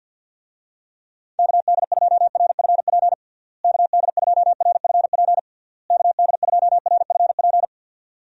Morse code, heard as a keyed tone, about 49 wpm, 700 Hz. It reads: KD2RFP KD2RFP KD2RFP
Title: USA Call Signs - 6 Characters - Repeated 3x - 50wpm